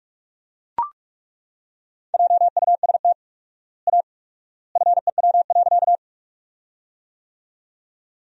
Morse code, heard as a keyed tone, about 42 wpm, 700 Hz. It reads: JUST A FEW.
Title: 4x Word-Spacing - Sentences from Top 300 Words - 45wpm